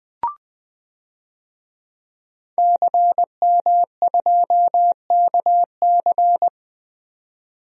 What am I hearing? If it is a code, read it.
Morse code, heard as a keyed tone, about 20 wpm, 700 Hz. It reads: CM2KC
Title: DXCC-100 Call Signs - 5 Characters - 20wpm